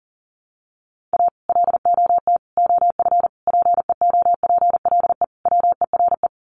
Morse code, heard as a keyed tone, about 40 wpm, 700 Hz. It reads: A LOT OF PEOPLE WERE
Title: Sets of 5 Words Encore2 40wpm